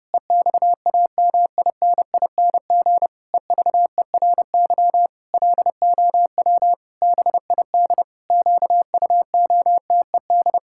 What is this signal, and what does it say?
Morse code, heard as a keyed tone, about 30 wpm, 700 Hz. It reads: EXAMINING EVERY LOW BID QUOTED
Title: Pangram Challenge v2 - 30wpm